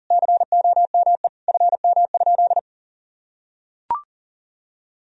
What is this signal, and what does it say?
Morse code, heard as a keyed tone, about 40 wpm, 700 Hz. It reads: COME FM?
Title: Sentences from Top 200 Words and Abbreviations 40wpm